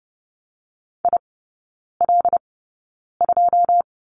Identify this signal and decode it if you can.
Morse code, heard as a keyed tone, about 30 wpm, 700 Hz. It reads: I L 2
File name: 081 Lesson 14e - Number 2 - Three Characters 30wpm